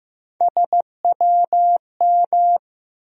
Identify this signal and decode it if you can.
Morse code, heard as a keyed tone, about 15 wpm, 700 Hz. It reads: SWM
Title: USA Call Sign Postfixes - 3 Letters 15wpm